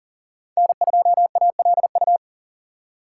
Morse code, heard as a keyed tone, about 37 wpm, 700 Hz. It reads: N2ALU
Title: USA Call Signs - 5 Characters - 2x Character Spacing - 40wpm